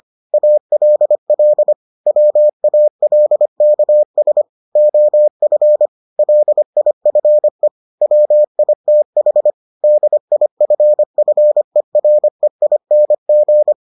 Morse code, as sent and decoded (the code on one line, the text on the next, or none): .- .-.. .-.. / .-- .- .-.. -.- ... / --- ..-. / .-.. .. ..-. / .-- .. - .... / -.. .. ..-. ..-. .-. .. -. --.
ALL WALKS OF LIFE WITH DIFFEREING